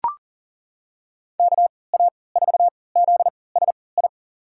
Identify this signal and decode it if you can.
Morse code, heard as a keyed tone, about 39 wpm, spaced as for 23 wpm, 700 Hz. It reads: KA4ZSI
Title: USA Call Signs - 6 Characters - 3x Character Spacing - 40wpm